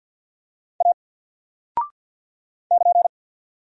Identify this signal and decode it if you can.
Morse code, heard as a keyed tone, about 50 wpm, 700 Hz. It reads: A <KN>
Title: Single Character (Letter:Number:Punctuation:Prosign) 50wpm